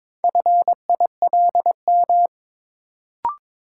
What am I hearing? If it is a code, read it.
Morse code, heard as a keyed tone, about 22 wpm, 700 Hz. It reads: FILM
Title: Sentences from Top 1000 Words - Review Part 3 - New Word Drill - 22wpm